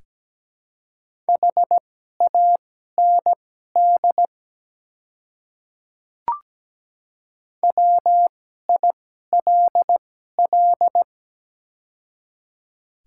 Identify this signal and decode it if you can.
Morse code, heard as a keyed tone, about 17 wpm, spaced as for 12 wpm, 700 Hz. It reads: HAND WILL
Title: 2x Character Spacing - Top 200 Words - 17wpm